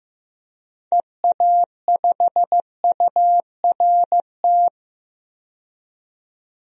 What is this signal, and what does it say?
Morse code, heard as a keyed tone, about 15 wpm, 700 Hz. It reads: EA5URT
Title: DXCC-100 Call Signs - 6 Characters - 15wpm